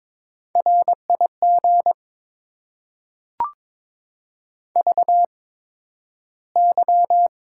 Morse code, heard as a keyed tone, about 21 wpm, spaced as for 12 wpm, 700 Hz. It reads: RIG VY
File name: Top 90 QSO Elements - 8x Character Spacing - 22wpm